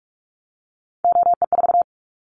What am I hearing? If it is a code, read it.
Morse code, heard as a keyed tone, about 44 wpm, 700 Hz. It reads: OE4
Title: Callsign Prefixes for Top 100 DXCC - 45wpm